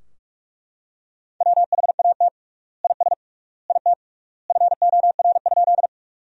Morse code, heard as a keyed tone, about 41 wpm, 700 Hz. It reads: WHAT IS IT FOR?
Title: 3x Word-Spacing - Sentences from Top 100 Words - 45wpm